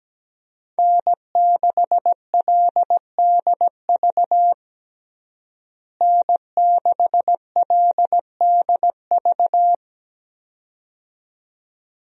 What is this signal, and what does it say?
Morse code, heard as a keyed tone, about 17 wpm, 700 Hz. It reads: N6LDV N6LDV
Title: USA Call Signs - 5 Characters - Repeated 2x - 17wpm